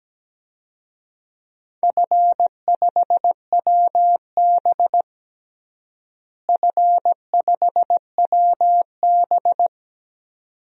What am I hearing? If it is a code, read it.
Morse code, heard as a keyed tone, about 17 wpm, 700 Hz. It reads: F5WB F5WB
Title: DXCC-100 Call Signs - 4 Characters - Repeated 2x - 17wpm